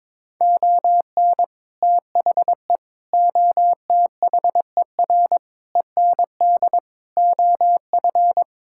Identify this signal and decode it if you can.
Morse code, heard as a keyed tone, about 21 wpm, 700 Hz. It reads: ON THE OTHER END OF
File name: practice / 22wpm / Sets of 7 Words - 22wpm